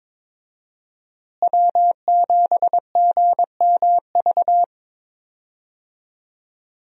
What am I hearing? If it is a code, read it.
Morse code, heard as a keyed tone, about 22 wpm, 700 Hz. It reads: W7GMV
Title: USA Call Signs - 5 Characters - Repeated 2x - 22wpm